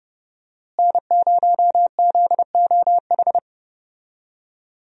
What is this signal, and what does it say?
Morse code, heard as a keyed tone, about 31 wpm, 700 Hz. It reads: N0ZOH
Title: USA Call Signs - 5 Characters - Repeated 2x - 30wpm